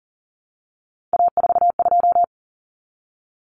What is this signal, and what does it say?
Morse code, heard as a keyed tone, about 40 wpm, 700 Hz. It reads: A42
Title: Common Cut Numbers - 3-Digit 40wpm